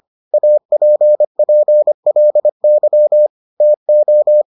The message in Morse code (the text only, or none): APPLY TO